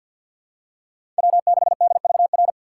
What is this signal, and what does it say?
Morse code, heard as a keyed tone, about 50 wpm, 700 Hz. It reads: W6DVR
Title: USA Call Signs - 2x Character Spacing - 50wpm